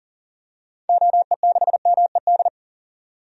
Morse code, heard as a keyed tone, about 38 wpm, 700 Hz. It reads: OE6MED
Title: DXCC-100 Call Signs - 6 Characters - 40wpm